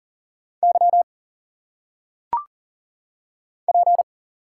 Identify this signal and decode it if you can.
Morse code, heard as a keyed tone, about 38 wpm, 700 Hz. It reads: Y P
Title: Single Letters - Phonetic Pronunciation 40wpm